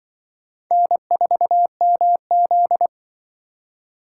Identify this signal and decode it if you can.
Morse code, heard as a keyed tone, about 26 wpm, 700 Hz. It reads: N4MZ